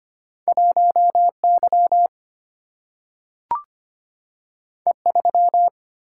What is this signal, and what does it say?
Morse code, heard as a keyed tone, about 25 wpm, 700 Hz. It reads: 1Y E3
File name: Two Letter-Number Combinations - 25wpm